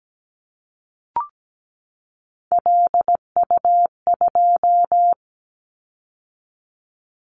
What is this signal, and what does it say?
Morse code, heard as a keyed tone, about 17 wpm, 700 Hz. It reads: LU2
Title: Callsign Prefixes for Top 100 DXCC - 17wpm